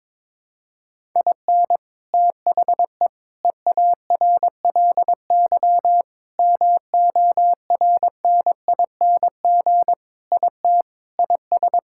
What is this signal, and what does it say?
Morse code, heard as a keyed tone, about 22 wpm, 700 Hz. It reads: IN THE EARLY MORNING IT IS
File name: practice / 22wpm / Sentences with 3-Word Phrases - 22wpm